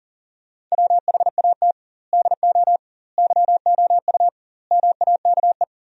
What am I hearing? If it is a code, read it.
Morse code, heard as a keyed tone, about 40 wpm, 700 Hz. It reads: WHAT DO YOU MAKE
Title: 2x Word-Spacing - Sentences from Top 100 Words - 40wpm